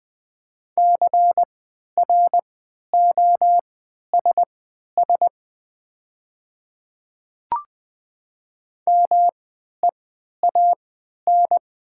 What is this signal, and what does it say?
Morse code, heard as a keyed tone, about 20 wpm, spaced as for 11 wpm, 700 Hz. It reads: CROSS MEAN
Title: 3x Character Spacing - Top 300 Words - 20wpm